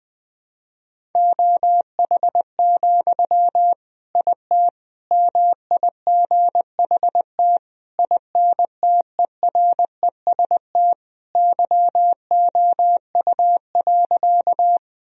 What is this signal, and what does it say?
Morse code, heard as a keyed tone, about 20 wpm, 700 Hz. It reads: OH, IT MIGHT INTEREST YOU.
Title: Sentences from Top 1000 Words - Review Part 2 - 20wpm